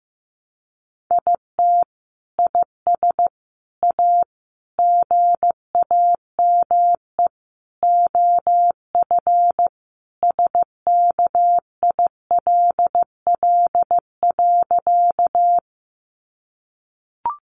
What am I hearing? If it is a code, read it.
Morse code, heard as a keyed tone, about 15 wpm, 700 Hz. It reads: IT IS A GAME OF SKILL.
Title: Sentences from Top 1200 Words 15wpm